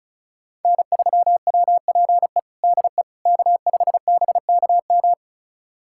Morse code, heard as a keyed tone, about 34 wpm, 700 Hz. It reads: N3WPE DE K5BKM
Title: USA Callsign DE Callsign - 5 Characters - 35wpm